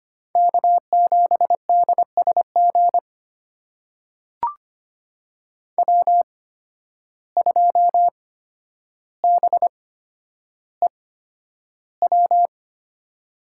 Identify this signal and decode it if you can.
Morse code, heard as a keyed tone, about 25 wpm, spaced as for 11 wpm, 700 Hz. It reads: K7DSG W2BEW
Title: USA Call Signs - 5 Characters - 8x Character Spacing - 25wpm